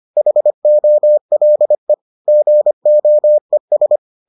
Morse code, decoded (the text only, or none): HOLE GOES